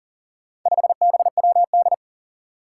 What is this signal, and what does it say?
Morse code, heard as a keyed tone, about 40 wpm, 700 Hz. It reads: 5BWD
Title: Four Challenging Letter-Number Combinations 40wpm